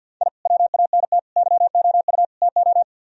Morse code, heard as a keyed tone, about 62 wpm, 700 Hz. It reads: I WANT YOU TO
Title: Speed-Racing - Sets of 4 Words 50wpm